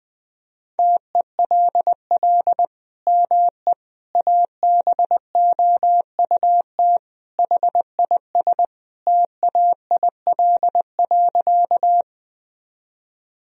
Morse code, heard as a keyed tone, about 20 wpm, 700 Hz. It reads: TELL ME ABOUT HIS TAIL.
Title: Sentences from Top 1000 Words - Review Part 3 - 20wpm